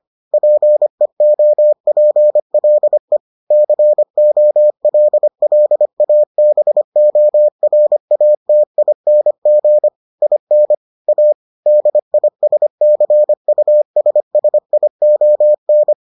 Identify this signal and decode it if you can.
Morse code, heard as a keyed tone, about 25 wpm, 600 Hz. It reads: PEOPLE COLLABORATING IN A DISCUSSION